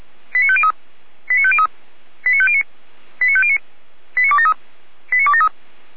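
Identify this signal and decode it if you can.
Five-tone selective call (ZVEI), heard as quick runs of five tones, tones of about 70 ms. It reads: 89503 89503 89509 89509 89273 89273
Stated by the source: ZVEI five-tone selective call received by SDR.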